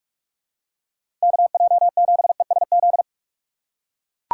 Morse code, heard as a keyed tone, about 42 wpm, 700 Hz. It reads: KJ7ESZ
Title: USA Call Signs - 2x Character Spacing - 45wpm